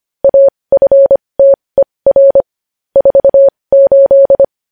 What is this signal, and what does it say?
Morse code, heard as a keyed tone, about 25 wpm, spaced as for 20 wpm, 550 Hz. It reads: AFTER 48
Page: News Headlines in Morse Code at 20 WPM